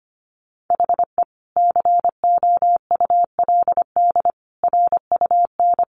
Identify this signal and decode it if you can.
Morse code, heard as a keyed tone, about 25 wpm, 700 Hz. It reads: HE COULD RUN
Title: Sentences from Top 1100 Words 25wpm